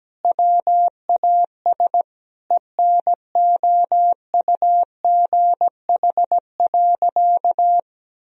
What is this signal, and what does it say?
Morse code, heard as a keyed tone, about 17 wpm, 700 Hz. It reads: WAS ENOUGH.
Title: Sentences from Top 1000 Words - Review Part 2 - 17wpm